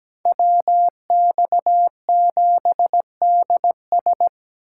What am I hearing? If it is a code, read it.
Morse code, heard as a keyed tone, about 20 wpm, 700 Hz. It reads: WX7DS
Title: Speed Racing - USA 2x2 Call Signs 17wpm